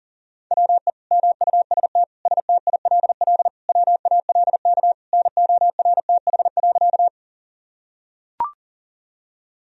Morse code, heard as a keyed tone, about 40 wpm, 700 Hz. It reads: WE MUST STILL WALK NORTH.
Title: Sentences from Top 1000 Words - Review Part 1 - 40wpm